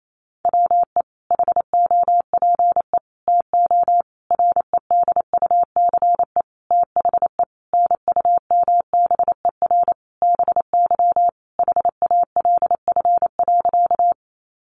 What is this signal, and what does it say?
Morse code, heard as a keyed tone, about 28 wpm, 700 Hz. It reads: WE HOPE TO REDUCE THE NUMBER BY HALF.
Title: Sentences from Top 1100 Words 28wpm